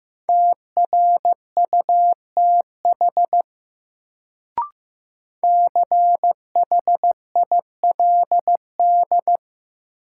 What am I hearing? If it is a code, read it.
Morse code, heard as a keyed tone, about 15 wpm, 700 Hz. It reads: TRUTH CHILD
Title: Sentences from Top 800 Words - New Word Drill - 15wpm